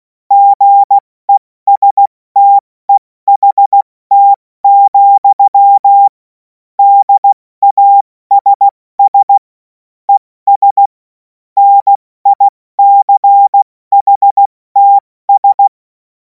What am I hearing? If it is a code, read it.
Morse code, heard as a keyed tone, about 16 wpm, 800 Hz. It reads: GESTEHT, DASS ES NICHTS
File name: Morsekurs / Witze / Witz 03. März 2026 70 F80